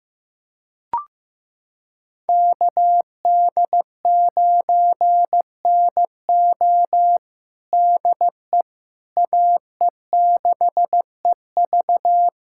Morse code, heard as a keyed tone, about 15 wpm, 700 Hz. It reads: KD9NO DE AE6EV
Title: USA Callsign DE Callsign - 5 Characters - 15wpm